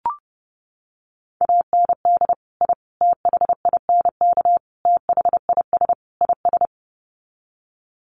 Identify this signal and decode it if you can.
Morse code, heard as a keyed tone, about 30 wpm, 700 Hz. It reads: AND I THINK THIS IS
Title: Sets of 5 Words Encore 30wpm